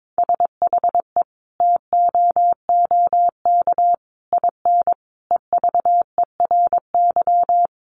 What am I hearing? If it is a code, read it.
Morse code, heard as a keyed tone, about 22 wpm, 700 Hz. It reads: SHE TOOK IN EVERY
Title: Sentences from Top 1500 Words 22wpm